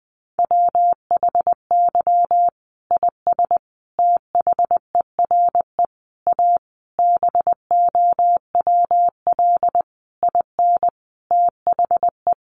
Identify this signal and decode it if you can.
Morse code, heard as a keyed tone, about 20 wpm, 700 Hz. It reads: WHY IS THERE A BOWL IN THE
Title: Sentences from Top 1200 Words 20wpm